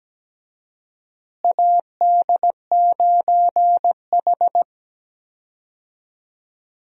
Morse code, heard as a keyed tone, about 17 wpm, 700 Hz. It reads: AD9H
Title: USA Call Signs - 4 Characters - Repeated 3x - 17wpm